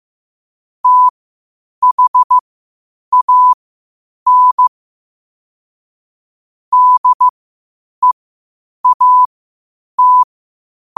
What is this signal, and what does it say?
Morse code, heard as a keyed tone, about 15 wpm, spaced as for 8 wpm, 1000 Hz. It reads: THAN DEAT
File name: qotdfiles / 8wpm